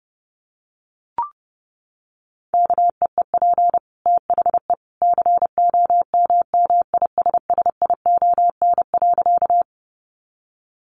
Morse code, heard as a keyed tone, about 30 wpm, 700 Hz. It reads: KEEP THE COMMISSION.
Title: Sentences from Top 1400 Words 30wpm